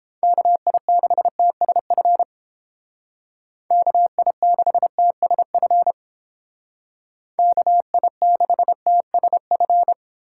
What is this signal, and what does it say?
Morse code, heard as a keyed tone, about 30 wpm, 700 Hz. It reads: KI6TSF KI6TSF KI6TSF